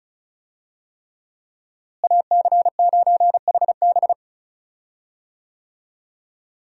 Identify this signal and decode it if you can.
Morse code, heard as a keyed tone, about 36 wpm, 700 Hz. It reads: AC9HB
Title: USA Call Signs - 5 Characters - Repeated 3x - 35wpm